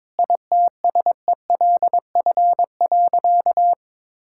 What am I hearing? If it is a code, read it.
Morse code, heard as a keyed tone, about 22 wpm, 700 Hz. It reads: ITSELF.
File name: Sentences with 9-Word Phrases - 22wpm